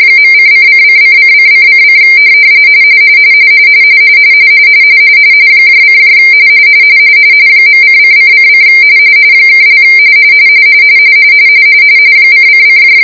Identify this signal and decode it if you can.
RTTY, about 45.45 baud. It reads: .:::::::::: : .::::: () :::